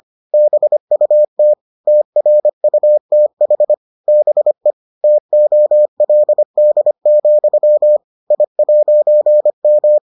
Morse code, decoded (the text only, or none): BUT TRUTH BE TOLD, I'M